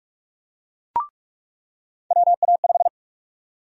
Morse code, heard as a keyed tone, about 43 wpm, 700 Hz. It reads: WA5
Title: USA Call Sign Prefixes - 45wpm